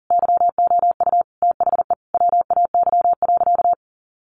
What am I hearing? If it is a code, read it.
Morse code, heard as a keyed tone, about 40 wpm, 700 Hz. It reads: YOU THE WAY.
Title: Sentences from Top 300 Words 40wpm